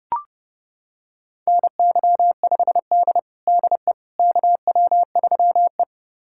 Morse code, heard as a keyed tone, about 30 wpm, 700 Hz. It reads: NY5D DE KW3E
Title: USA Callsign DE Callsign - 4 Characters - 30wpm